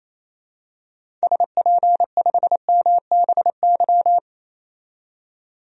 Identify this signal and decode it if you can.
Morse code, heard as a keyed tone, about 28 wpm, 700 Hz. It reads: SP5MBY
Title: DXCC-100 Call Signs - 6 Characters - 28wpm